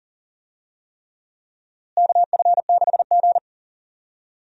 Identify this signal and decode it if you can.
Morse code, heard as a keyed tone, about 40 wpm, 700 Hz. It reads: KF6G
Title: USA Call Signs - 4 Characters - Repeated 3x - 40wpm